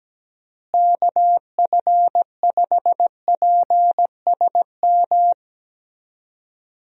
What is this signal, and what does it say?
Morse code, heard as a keyed tone, about 17 wpm, 700 Hz. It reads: KF5PSM
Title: USA Call Signs - 6 Characters - Repeated 2x - 17wpm